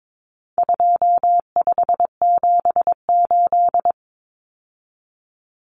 Morse code, heard as a keyed tone, about 22 wpm, 700 Hz. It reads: 2578
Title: Common Cut Numbers - 4-Digit 22wpm